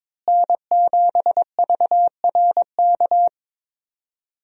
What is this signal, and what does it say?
Morse code, heard as a keyed tone, about 22 wpm, 700 Hz. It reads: N7VRK
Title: USA Call Signs - 5 Characters - Repeated 2x - 22wpm